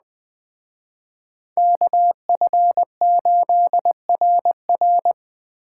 Morse code, heard as a keyed tone, about 21 wpm, 700 Hz. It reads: KF8RR